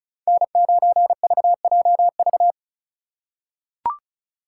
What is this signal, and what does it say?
Morse code, heard as a keyed tone, about 34 wpm, 700 Hz. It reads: N9VJV
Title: USA Call Signs - 5 Characters - 4x Character Spacing - 35wpm